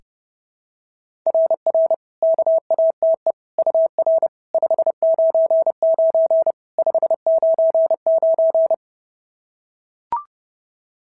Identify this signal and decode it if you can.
Morse code, heard as a keyed tone, about 30 wpm, 660 Hz. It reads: RR KATE UR 599 599
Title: SOTA QSO Practice 30wpm